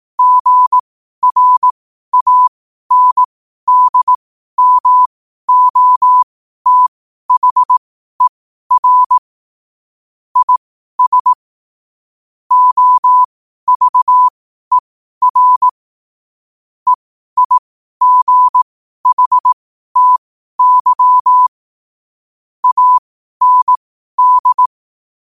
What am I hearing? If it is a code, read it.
Morse code, heard as a keyed tone, about 18 wpm, spaced as for 12 wpm, 1000 Hz. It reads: GRANDMOTHER IS OVER EIGHTY AND
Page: Quotes for Sun, 27 Apr 2025 in Morse Code at 12 words per minute.